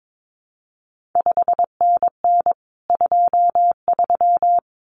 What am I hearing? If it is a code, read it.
Morse code, heard as a keyed tone, about 22 wpm, 700 Hz. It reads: 5NN 23
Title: CQ WW DX Contest Exchange Encore - 22wpm